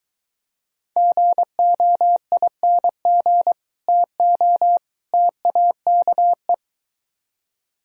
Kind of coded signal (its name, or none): Morse code